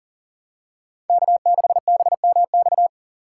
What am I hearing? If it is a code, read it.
Morse code, heard as a keyed tone, about 40 wpm, 700 Hz. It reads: K6BMX
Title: USA Call Signs - 5 Characters - 2x Character Spacing - 40wpm